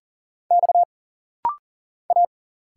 Morse code, heard as a keyed tone, about 40 wpm, 700 Hz. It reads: X A
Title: Single Letters - Rapid-Fire 40wpm